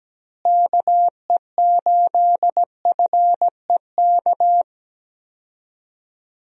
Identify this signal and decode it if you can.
Morse code, heard as a keyed tone, about 17 wpm, 700 Hz. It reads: KE8FEK
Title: USA Call Signs - 6 Characters - Repeated 3x - 17wpm